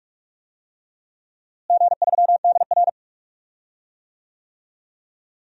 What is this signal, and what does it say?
Morse code, heard as a keyed tone, about 41 wpm, 690 Hz. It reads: G3DR
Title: DXCC-100 Call Signs - 4 Characters - Repeated 3x - 45wpm